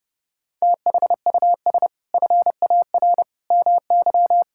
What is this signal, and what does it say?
Morse code, heard as a keyed tone, about 30 wpm, 700 Hz. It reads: THUS FAR MY
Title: Sentences from Top 1000 Words - Review Part 5 - 30wpm